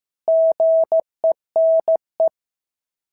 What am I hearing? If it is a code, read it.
Morse code, heard as a keyed tone, about 14 wpm, 650 Hz. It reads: GENE
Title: SOTA QSO Practice 15wpm